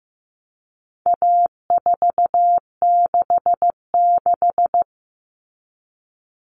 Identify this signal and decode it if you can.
Morse code, heard as a keyed tone, about 15 wpm, 700 Hz. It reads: A466
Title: Common Cut Numbers - 4-Digit 15wpm